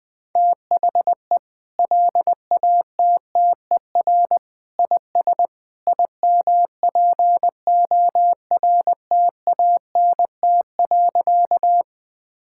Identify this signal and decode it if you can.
Morse code, heard as a keyed tone, about 21 wpm, 700 Hz. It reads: THE LATTER IS IMPORTANT.
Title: Sentences from Top 1600 Words 20wpm